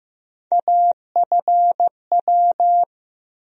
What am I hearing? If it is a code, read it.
Morse code, heard as a keyed tone, about 15 wpm, 700 Hz. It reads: AFW
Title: USA Call Sign Postfixes - 3 Letters 15wpm